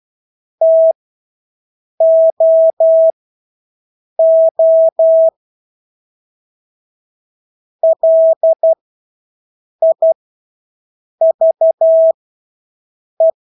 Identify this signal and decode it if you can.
Morse code, heard as a keyed tone, about 12 wpm, spaced as for 6 wpm, 650 Hz. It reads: TOO LIVE
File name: Engelske ord 12-6 wpm